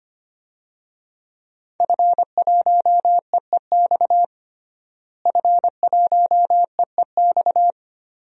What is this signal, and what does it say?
Morse code, heard as a keyed tone, about 25 wpm, 700 Hz. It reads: F1EEX F1EEX
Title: DXCC-100 Call Signs - 5 Characters - Repeated 2x - 25wpm